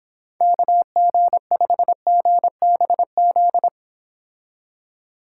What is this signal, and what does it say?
Morse code, heard as a keyed tone, about 26 wpm, 700 Hz. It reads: KG5GBZ